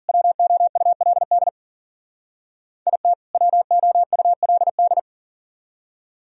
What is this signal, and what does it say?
Morse code, heard as a keyed tone, about 50 wpm, 700 Hz. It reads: WOULD IT WOULD